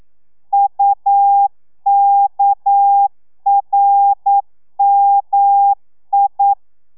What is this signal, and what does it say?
Morse code, heard as a keyed tone, about 9 wpm, 800 Hz. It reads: UKRMI
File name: telegrafia / i2tfj / Lez6 40